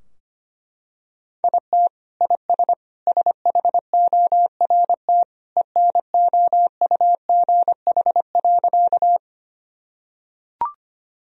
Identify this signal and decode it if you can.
Morse code, heard as a keyed tone, about 25 wpm, 700 Hz. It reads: IT IS SHORT ENOUGH.
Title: Sentences from Top 1000 Words - Review Part 2 - 25wpm